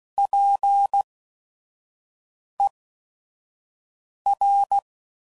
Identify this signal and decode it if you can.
Morse code, heard as a keyed tone, about 16 wpm, spaced as for 5 wpm, 800 Hz. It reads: PER
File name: CW Code Groups @5WPM / Technical Subjects / Reactance